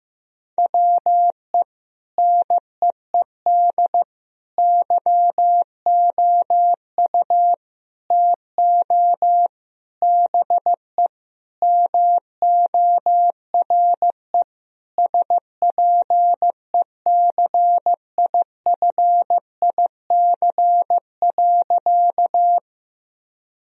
Morse code, heard as a keyed tone, about 15 wpm, 700 Hz. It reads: WE NEED YOU TO BE MORE SPECIFIC.
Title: Sentences from Top 1000 Words - Review Part 3 - 15wpm